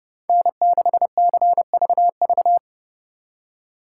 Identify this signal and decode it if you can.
Morse code, heard as a keyed tone, about 29 wpm, 700 Hz. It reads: N6CVV
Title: Speed Racing - USA Call Signs 30wpm